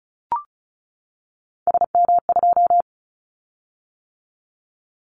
Morse code, heard as a keyed tone, about 36 wpm, 700 Hz. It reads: SM2
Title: Callsign Prefixes for Top 25 DXCC - 35wpm